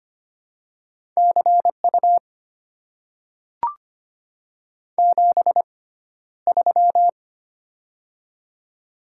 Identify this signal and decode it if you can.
Morse code, heard as a keyed tone, about 25 wpm, spaced as for 13 wpm, 700 Hz. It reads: CU 73
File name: Top 90 QSO Elements - 6x Character Spacing - 25wpm